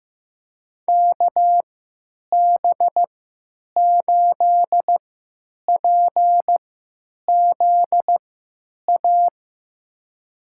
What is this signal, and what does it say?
Morse code, heard as a keyed tone, about 15 wpm, spaced as for 8 wpm, 700 Hz. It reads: KB8PZA
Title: USA Call Signs - 6 Characters - 3x Character Spacing - 15wpm